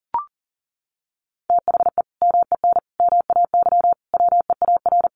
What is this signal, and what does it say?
Morse code, heard as a keyed tone, about 40 wpm, 700 Hz. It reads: THE MEN MAY WEAR
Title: Sentences from Top 1500 Words 40wpm